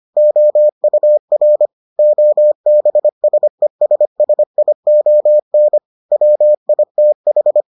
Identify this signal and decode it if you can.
Morse code, heard as a keyed tone, about 25 wpm, 600 Hz. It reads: OUR OBSESSION WITH